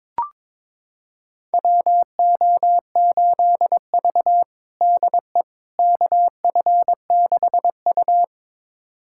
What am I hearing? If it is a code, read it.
Morse code, heard as a keyed tone, about 22 wpm, 700 Hz. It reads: WO8V DE KF6U
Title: USA Callsign DE Callsign - 4 Characters - 22wpm